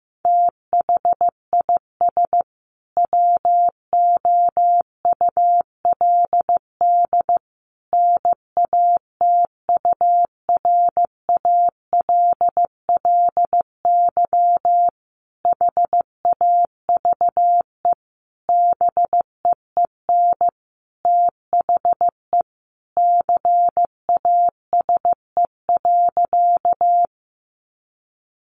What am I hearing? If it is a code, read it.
Morse code, heard as a keyed tone, about 15 wpm, 700 Hz. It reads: THIS WOULD NATURALLY HAVE BEEN THE CASE.
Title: Sentences from Top 1300 Words 15wpm